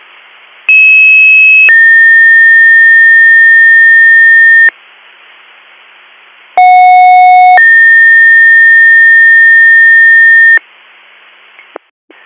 Two-tone page, heard as a one-second tone, then a three-second tone, about 2600 then 1800 Hz, about 730 then 1800 Hz. It is heard from 0.7 to 10.6 s.